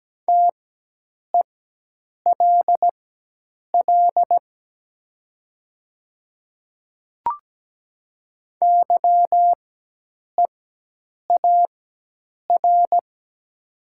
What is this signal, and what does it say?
Morse code, heard as a keyed tone, about 17 wpm, spaced as for 8 wpm, 700 Hz. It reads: TELL YEAR
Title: 4x Character Spacing - Top 300 Words - 17wpm